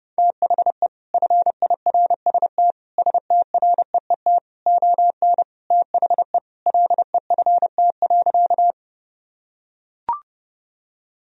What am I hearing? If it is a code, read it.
Morse code, heard as a keyed tone, about 30 wpm, 700 Hz. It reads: THE FIRST STREET ON THE LEFT.
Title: Sentences from Top 1000 Words - Review Part 3 - 30wpm